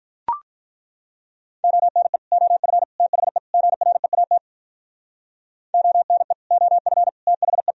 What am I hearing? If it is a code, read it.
Morse code, heard as a keyed tone, about 43 wpm, 700 Hz. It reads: ONE OF THE GREAT ONE OF THE
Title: Speed-Racing - Sets of 4 Words 35wpm